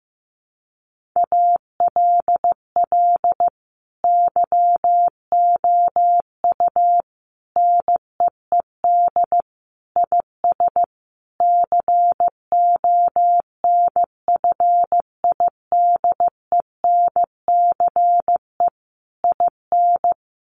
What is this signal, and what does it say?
Morse code, heard as a keyed tone, about 15 wpm, 700 Hz. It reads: ALL YOU NEED IS CONFIDENCE IN
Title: Sentences from Top 1500 Words 15wpm